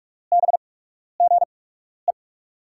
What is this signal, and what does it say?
Morse code, heard as a keyed tone, about 42 wpm, spaced as for 12 wpm, 700 Hz. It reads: BGE
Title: USA Call Signs - 6 Characters - 8x Character Spacing - 45wpm